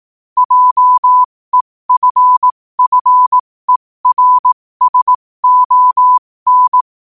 Morse code, heard as a keyed tone, about 18 wpm, spaced as for 15 wpm, 1000 Hz. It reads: JEFFERSON
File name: qotdfiles / 15wpm